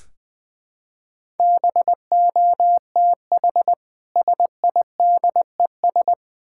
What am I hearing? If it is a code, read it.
Morse code, heard as a keyed tone, about 20 wpm, 700 Hz. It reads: BOTH SIDES
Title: Build A Phrase - 4-Word Phrases - 20wpm